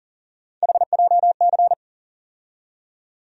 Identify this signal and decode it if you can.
Morse code, heard as a keyed tone, about 40 wpm, 700 Hz. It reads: HJC
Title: USA Call Sign Postfixes - 3 Letters 40wpm